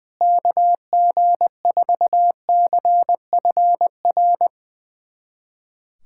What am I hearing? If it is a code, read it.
Morse code, heard as a keyed tone, about 21 wpm, 700 Hz. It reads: KG4CFR
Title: USA Callsign DE Callsign - 6 Characters - 20wpm